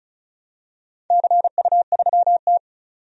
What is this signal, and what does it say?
Morse code, heard as a keyed tone, about 34 wpm, 700 Hz. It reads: CU3T
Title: DXCC-100 Call Signs - 4 Characters - 35wpm